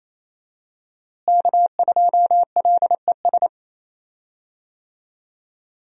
Morse code, heard as a keyed tone, about 28 wpm, 700 Hz. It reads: K2LES
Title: USA Call Signs - 5 Characters - Repeated 2x - 28wpm